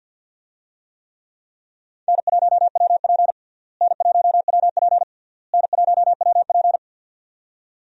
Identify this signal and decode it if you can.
Morse code, heard as a keyed tone, about 48 wpm, 700 Hz. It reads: N1WP N1WP N1WP
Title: USA Call Signs - 4 Characters - Repeated 3x - 50wpm